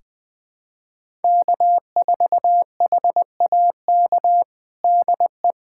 Morse code, heard as a keyed tone, about 20 wpm, 700 Hz. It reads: K4HAK DE
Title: USA Callsign DE Callsign 20wpm